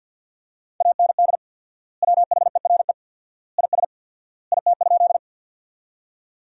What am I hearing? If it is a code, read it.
Morse code, heard as a keyed tone, about 50 wpm, 700 Hz. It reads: AND WHERE IS IT?
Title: 4x Word-Spacing - Sentences from Top 200 Words - 50wpm